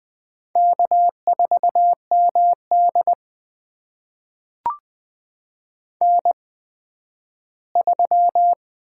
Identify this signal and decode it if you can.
Morse code, heard as a keyed tone, about 20 wpm, spaced as for 12 wpm, 700 Hz. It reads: K4MD N3
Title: USA Call Signs - 4 Characters - 8x Character Spacing - 20wpm